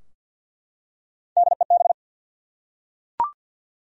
Morse code, heard as a keyed tone, about 52 wpm, 700 Hz. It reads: DEB
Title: USA Common Names - Max 3 Letters - 50wpm